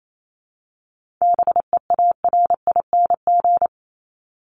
Morse code, heard as a keyed tone, about 28 wpm, 700 Hz. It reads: BEARING
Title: ING Suffix - Top 100 Words with Base Words 28wpm